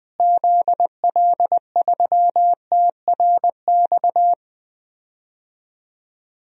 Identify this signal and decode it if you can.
Morse code, heard as a keyed tone, about 20 wpm, 700 Hz. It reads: ZL3TRX
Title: DXCC-100 Call Signs - 6 Characters - Repeated 3x - 20wpm